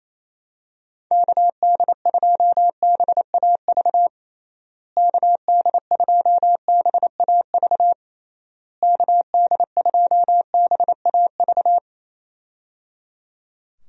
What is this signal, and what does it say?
Morse code, heard as a keyed tone, about 28 wpm, 700 Hz. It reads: KD2BAV KD2BAV KD2BAV
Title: USA Call Signs - 6 Characters - Repeated 3x - 28wpm